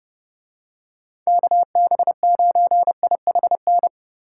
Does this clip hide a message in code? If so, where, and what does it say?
Yes: Morse code from 1.3 to 3.9 s, heard as a keyed tone, about 28 wpm, 700 Hz. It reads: KB9IHN